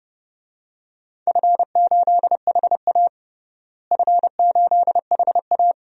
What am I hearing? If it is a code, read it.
Morse code, heard as a keyed tone, about 30 wpm, 700 Hz. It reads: F8HA F8HA
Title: DXCC-100 Call Signs - 4 Characters - Repeated 2x - 30wpm